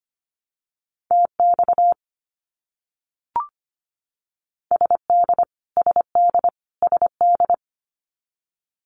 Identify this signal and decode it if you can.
Morse code, heard as a keyed tone, about 25 wpm, 700 Hz. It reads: TX SD SD SD
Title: US States Repeated 3x 25wpm